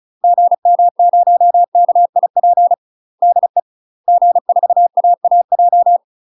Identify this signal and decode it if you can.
Morse code, heard as a keyed tone, about 36 wpm, 700 Hz. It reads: GM0KIP DE G4AAJ
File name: UK MASTER QSOs TRAINING/35WPM